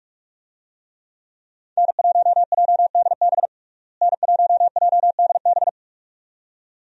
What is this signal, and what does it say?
Morse code, heard as a keyed tone, about 46 wpm, 700 Hz. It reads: N1JDB N1JDB
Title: USA Call Signs - 5 Characters - Repeated 2x - 45wpm